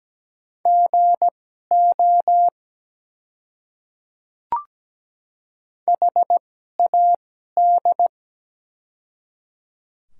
Morse code, heard as a keyed tone, about 17 wpm, spaced as for 12 wpm, 700 Hz. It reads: GO HAD
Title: 2x Character Spacing - Top 100 Words - 17wpm